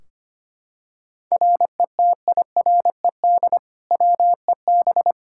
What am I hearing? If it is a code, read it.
Morse code, heard as a keyed tone, about 25 wpm, 700 Hz. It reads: RETIRED WEB
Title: K1USN SST QSO Practice 25wpm